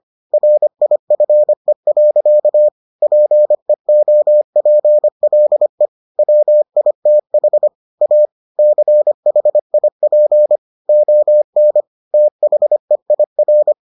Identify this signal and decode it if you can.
Morse code, heard as a keyed tone, about 25 wpm, 600 Hz. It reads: RIFE. PEOPLE WITH A CHIP ON THEIR